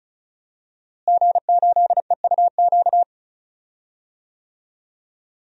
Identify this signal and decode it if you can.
Morse code, heard as a keyed tone, about 36 wpm, 700 Hz. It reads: G8EUQ
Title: DXCC-100 Call Signs - 5 Characters - Repeated 3x - 35wpm